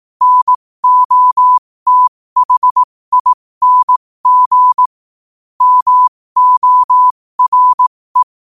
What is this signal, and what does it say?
Morse code, heard as a keyed tone, about 18 wpm, spaced as for 15 wpm, 1000 Hz. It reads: NOTHING MORE
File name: qotdfiles / 15wpm